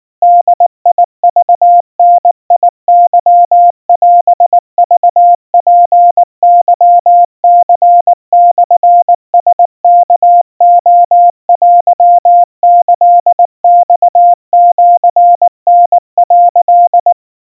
Morse code (Japanese Wabun code), heard as a keyed tone, about 19 wpm, 700 Hz. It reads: ボクダケオクツケニモラワレテキマシタ」
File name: kawa / Morse sample / kamyu 70 cpm 700 Hz Crispy